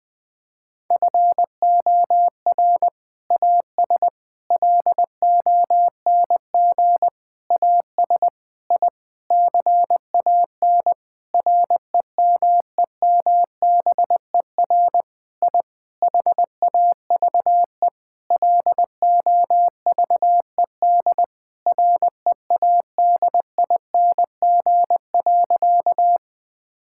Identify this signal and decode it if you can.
Morse code, heard as a keyed tone, about 20 wpm, 700 Hz. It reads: FOR AS LONG AS I CAN REMEMBER I HAVE LOVED READING.
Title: Sentences with 8-Word Phrases - 20wpm